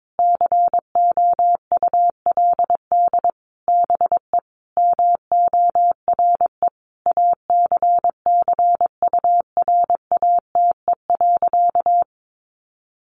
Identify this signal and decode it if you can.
Morse code, heard as a keyed tone, about 22 wpm, 700 Hz. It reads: COULD BE MORE ACCURATE.
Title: Sentences from Top 1200 Words 22wpm